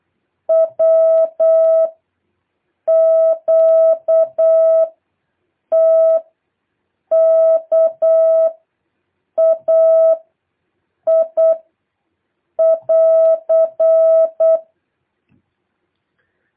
Morse code, heard as a keyed tone, about 8 wpm, 650 Hz. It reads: W Q T K A I <AR>
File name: jh8cbh / 2023cw